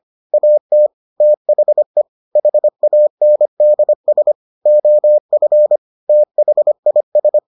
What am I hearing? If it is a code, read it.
Morse code, heard as a keyed tone, about 25 wpm, 600 Hz. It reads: AT THE HANDS OF THIS